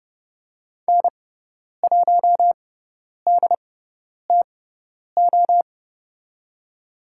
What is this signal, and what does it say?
Morse code, heard as a keyed tone, about 30 wpm, spaced as for 10 wpm, 700 Hz. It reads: N1DTO
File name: Lesson 16g - Letter c - Call Signs 30wpm